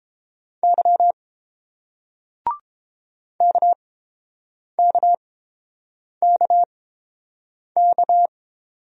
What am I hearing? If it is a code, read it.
Morse code, heard as a keyed tone, about 31 wpm, spaced as for 7 wpm, 690 Hz. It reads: Y KKKK